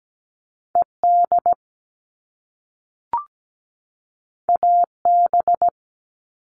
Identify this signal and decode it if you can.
Morse code, heard as a keyed tone, about 17 wpm, 700 Hz. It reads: ED AB
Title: All Cut Numbers - 2-Digit 17wpm